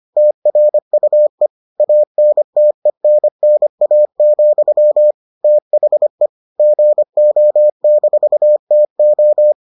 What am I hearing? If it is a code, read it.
Morse code, heard as a keyed tone, about 25 wpm, 600 Hz. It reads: TRUE ANTENNA, THE GO-TO